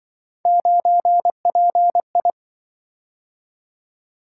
Morse code, heard as a keyed tone, about 26 wpm, 700 Hz. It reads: 9PI